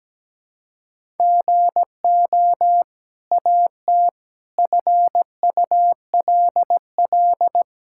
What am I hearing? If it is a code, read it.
Morse code, heard as a keyed tone, about 17 wpm, 700 Hz. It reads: GO AT FULL
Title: Sentences from Top 1000 Words - Review Part 5 - 17wpm